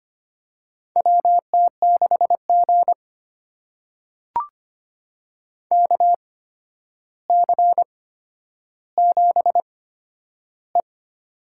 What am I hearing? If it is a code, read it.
Morse code, heard as a keyed tone, about 25 wpm, spaced as for 11 wpm, 700 Hz. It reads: WT6G KC7E
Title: USA Call Signs - 4 Characters - 8x Character Spacing - 25wpm